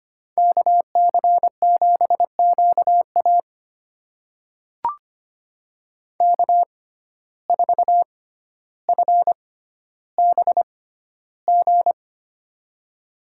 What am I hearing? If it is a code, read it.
Morse code, heard as a keyed tone, about 25 wpm, spaced as for 13 wpm, 700 Hz. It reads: KC7QA K4FBG
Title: USA Call Signs - 5 Characters - 6x Character Spacing - 25wpm